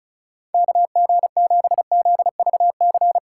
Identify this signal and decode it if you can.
Morse code, heard as a keyed tone, about 44 wpm, 700 Hz. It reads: KG7ZVC
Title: Speed Racing - USA 2x3 Call Signs 35wpm